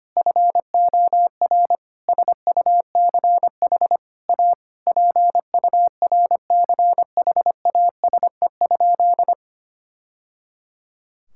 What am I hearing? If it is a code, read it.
Morse code, heard as a keyed tone, about 25 wpm, 700 Hz. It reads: FOR SUCH A PURCHASE?
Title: Sentences from Top 1000 Words - Review Part 6 - 25wpm